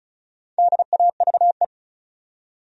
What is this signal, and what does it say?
Morse code, heard as a keyed tone, about 36 wpm, 700 Hz. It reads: DAVE
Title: USA Common Names - Max 4 Letters - 35wpm